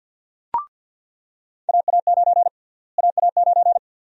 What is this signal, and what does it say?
Morse code, heard as a keyed tone, about 46 wpm, 700 Hz. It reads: AA9 AA9
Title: USA Call Sign Prefixes - Repeated 2x - 50wpm